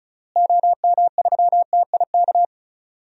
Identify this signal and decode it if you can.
Morse code, heard as a keyed tone, about 34 wpm, 700 Hz. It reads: OM3TIK
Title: DXCC-100 Call Signs - 6 Characters - Repeated 3x - 35wpm